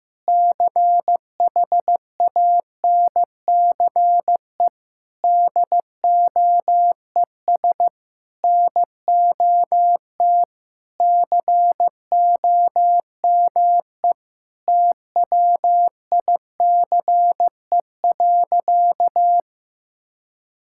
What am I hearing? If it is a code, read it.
Morse code, heard as a keyed tone, about 15 wpm, 700 Hz. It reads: CHANCE DOES NOT COME TWICE.
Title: Sentences from Top 1600 Words 15wpm